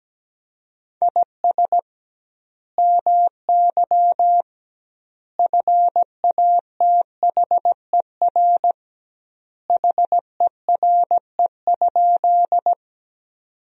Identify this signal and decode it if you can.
Morse code, heard as a keyed tone, about 17 wpm, 700 Hz. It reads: IS MY FATHER HERE?
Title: 2x Word-Spacing - Sentences from Top 300 Words - 17wpm